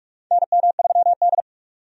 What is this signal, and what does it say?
Morse code, heard as a keyed tone, about 46 wpm, 700 Hz. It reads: NM3D
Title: USA Call Signs - 4 Characters - 45wpm